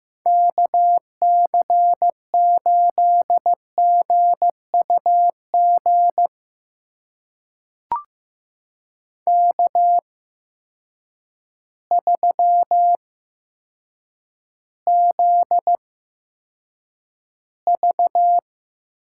Morse code, heard as a keyed tone, about 15 wpm, spaced as for 8 wpm, 700 Hz. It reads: KC8GUG K3ZV
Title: USA Call Signs - 8x Character Spacing - 15wpm